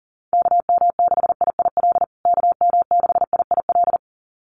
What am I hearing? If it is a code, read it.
Morse code, heard as a keyed tone, about 40 wpm, 700 Hz. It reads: KM6IIL KM6IIL
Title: USA Call Signs Repeated 2x 40wpm